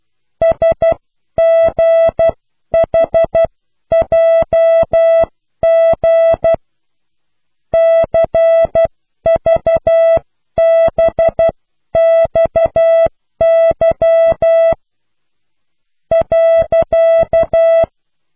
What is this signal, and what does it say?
Morse code, heard as a keyed tone, about 12 wpm, spaced as for 10 wpm, 660 Hz. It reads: SGHJG CVBXY .